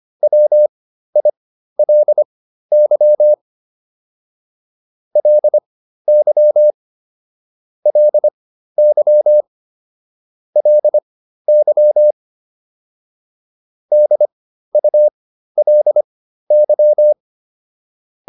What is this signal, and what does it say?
Morse code, heard as a keyed tone, about 25 wpm, spaced as for 13 wpm, 600 Hz. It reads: WILY LY LY LY DULY